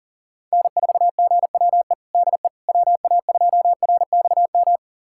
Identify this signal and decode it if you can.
Morse code, heard as a keyed tone, about 40 wpm, 700 Hz. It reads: N4GWE DE WA2RXM
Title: USA Callsign DE Callsign 40wpm